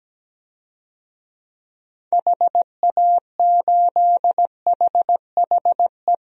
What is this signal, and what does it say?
Morse code, heard as a keyed tone, about 17 wpm, 700 Hz. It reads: HA8HHE
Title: DXCC-100 Call Signs - 6 Characters - Repeated 2x - 17wpm